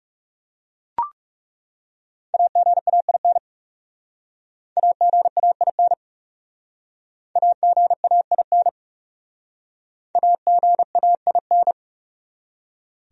Morse code, heard as a keyed tone, about 38 wpm, 700 Hz. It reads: AGAIN AGAIN AGAIN AGAIN
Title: Speed-Racing Top 200 Words 30wpm